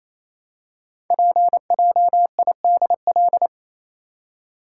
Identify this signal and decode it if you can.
Morse code, heard as a keyed tone, about 27 wpm, 700 Hz. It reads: PJIDL
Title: Code Groups - 5 Letters - 28wpm